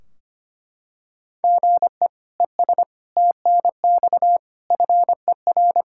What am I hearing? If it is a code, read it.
Morse code, heard as a keyed tone, about 25 wpm, 700 Hz. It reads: GE ES TNX FER
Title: Practice QSO - Exchange 1 - Beginning - 25wpm